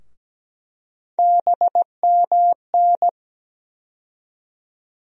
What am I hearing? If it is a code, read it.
Morse code, heard as a keyed tone, about 17 wpm, 700 Hz. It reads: BMN
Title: USA Call Sign Postfixes - 3 Letters 17wpm